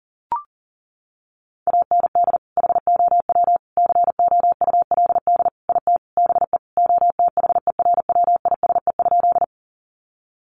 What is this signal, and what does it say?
Morse code, heard as a keyed tone, about 40 wpm, 700 Hz. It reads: AND HOW COULD IT BE OTHERWISE?
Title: Sentences from Top 1200 Words 40wpm